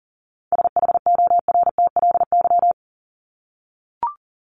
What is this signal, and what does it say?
Morse code, heard as a keyed tone, about 40 wpm, 700 Hz. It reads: SHORTLY
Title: LY Suffix - Top 100 Words with Base Words 40wpm